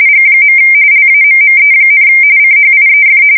RTTY, about 45.45 baud. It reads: G. LAMME, CHIEF